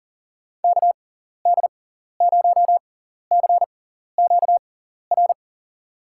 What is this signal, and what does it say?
Morse code, heard as a keyed tone, about 40 wpm, spaced as for 14 wpm, 700 Hz. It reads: KD0CQR
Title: USA Call Signs - 6 Characters - 6x Character Spacing - 40wpm